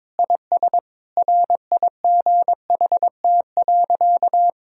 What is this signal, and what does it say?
Morse code, heard as a keyed tone, about 22 wpm, 700 Hz. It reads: IS RIGHT.
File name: practice / 22wpm / Sentences with 3-Word Phrases - 22wpm